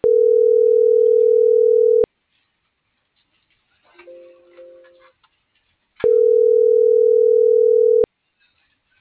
Ringback tone (North America), 440 + 480 Hz.